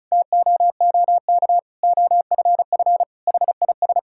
Morse code, heard as a keyed tone, about 34 wpm, 700 Hz. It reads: TOOK OFF HIS
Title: Sentences from Top 1600 Words 35wpm